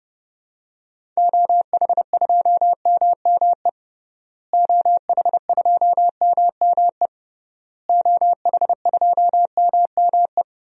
Morse code, heard as a keyed tone, about 32 wpm, 700 Hz. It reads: OH2MME OH2MME OH2MME
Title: DXCC-100 Call Signs - 6 Characters - Repeated 3x - 30wpm